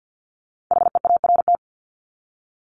Morse code, heard as a keyed tone, about 50 wpm, 700 Hz. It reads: HEART
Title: Top 1001-1250 Words 50wpm